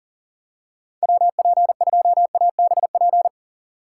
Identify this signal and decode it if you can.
Morse code, heard as a keyed tone, about 40 wpm, 700 Hz. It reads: WP2ABP
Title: USA Call Signs - 6 Characters - 4x Character Spacing - 40wpm